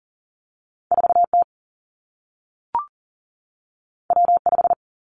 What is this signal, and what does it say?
Morse code, heard as a keyed tone, about 41 wpm, 700 Hz. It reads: 4 T W 5
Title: Two Challenging Letter-Number Combinations 40wpm